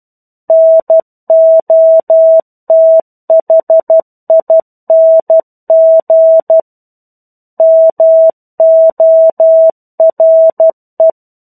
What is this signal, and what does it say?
Morse code, heard as a keyed tone, about 12 wpm, 650 Hz. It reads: NOTHING MORE
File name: 01-Chapter-01 12 WPM